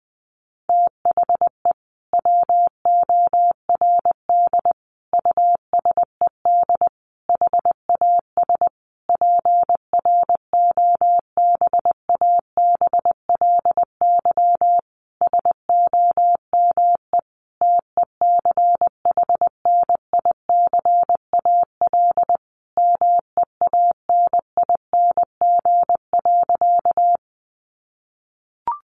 Morse code, heard as a keyed tone, about 20 wpm, 700 Hz. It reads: THE WORD USED HAS PROBABLY SOME TECHNICAL MEANING.
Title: Sentences from Top 1100 Words 20wpm